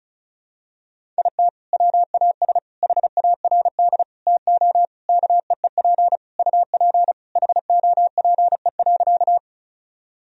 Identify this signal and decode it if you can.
Morse code, heard as a keyed tone, about 36 wpm, 700 Hz. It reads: IT WAS HARD TO KEEP UP HOPE.
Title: Sentences from Top 1000 Words - Review Part 1 - 35wpm